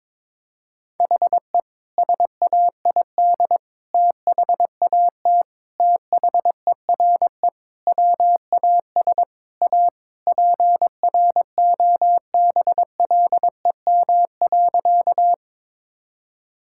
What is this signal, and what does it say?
Morse code, heard as a keyed tone, about 22 wpm, 700 Hz. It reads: HE SAID THAT THERE WAS A PROBLEM.
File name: practice / 22wpm / Sentences with 3-Word Phrases - 22wpm